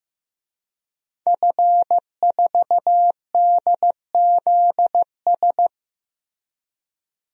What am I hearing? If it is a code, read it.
Morse code, heard as a keyed tone, about 15 wpm, 700 Hz. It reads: F4DZS
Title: DXCC-100 Call Signs - 5 Characters - Repeated 3x - 15wpm